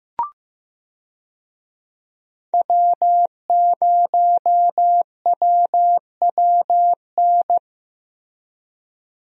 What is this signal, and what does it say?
Morse code, heard as a keyed tone, about 15 wpm, 700 Hz. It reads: W0WWN
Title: USA Call Signs - 5 Characters - Repeated 3x - 15wpm